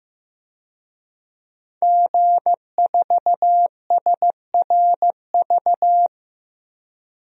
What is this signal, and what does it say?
Morse code, heard as a keyed tone, about 15 wpm, 700 Hz. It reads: G4SRV
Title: DXCC-100 Call Signs - 5 Characters - Repeated 3x - 15wpm